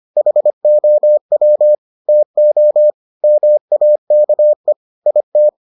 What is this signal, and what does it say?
Morse code, heard as a keyed tone, about 25 wpm, 600 Hz. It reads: HOW TO MAKE IT